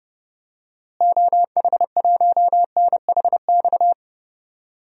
Morse code, heard as a keyed tone, about 30 wpm, 700 Hz. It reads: OH1NHX
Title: DXCC-100 Call Signs - 6 Characters - Repeated 3x - 30wpm